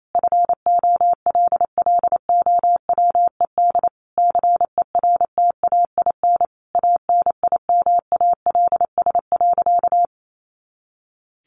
Morse code, heard as a keyed tone, about 28 wpm, 700 Hz. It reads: FOLLOWED CERTAIN ANIMALS.
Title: A Sound Of Thunder 28wpm